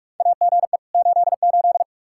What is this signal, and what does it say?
Morse code, heard as a keyed tone, about 41 wpm, 700 Hz. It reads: AGE 88
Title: Practice QSO - Exchange 3 - More Info - 45wpm